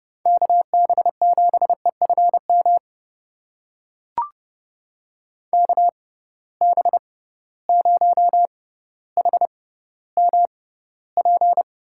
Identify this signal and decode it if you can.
Morse code, heard as a keyed tone, about 32 wpm, spaced as for 16 wpm, 700 Hz. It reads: KB7EFM KB0HMP
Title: USA Call Signs - 6 Characters - 6x Character Spacing - 30wpm